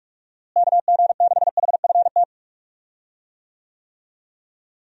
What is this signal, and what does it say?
Morse code, heard as a keyed tone, about 46 wpm, 700 Hz. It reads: KG6HFT
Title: USA Call Signs - 6 Characters - Repeated 3x - 45wpm